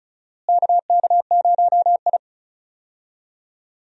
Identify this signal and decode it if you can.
Morse code, heard as a keyed tone, about 36 wpm, 700 Hz. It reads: KK0I
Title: USA Call Signs - 4 Characters - Repeated 3x - 35wpm